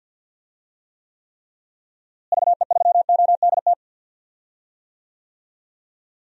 Morse code, heard as a keyed tone, about 50 wpm, 700 Hz. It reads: VE3ODT
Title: DXCC-100 Call Signs - 6 Characters - Repeated 3x - 50wpm